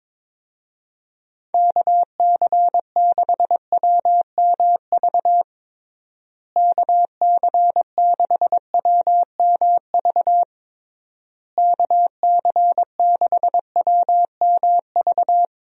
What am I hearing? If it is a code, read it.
Morse code, heard as a keyed tone, about 22 wpm, 700 Hz. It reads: KC6WMV KC6WMV KC6WMV
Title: USA Call Signs - 6 Characters - Repeated 3x - 22wpm